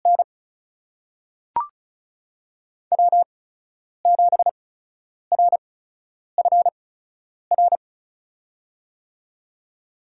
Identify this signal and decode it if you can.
Morse code, heard as a keyed tone, about 36 wpm, spaced as for 10 wpm, 700 Hz. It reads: N W7RFR
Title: USA Call Signs - 5 Characters - 8x Character Spacing - 35wpm